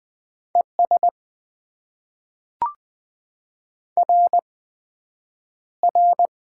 Morse code, heard as a keyed tone, about 20 wpm, spaced as for 9 wpm, 700 Hz. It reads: ES RR
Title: Top 90 QSO Elements - 8x Character Spacing - 20wpm